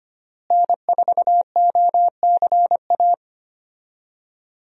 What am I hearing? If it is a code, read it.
Morse code, heard as a keyed tone, about 25 wpm, 700 Hz. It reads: N4OCA
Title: USA Call Signs - 5 Characters - Repeated 3x - 25wpm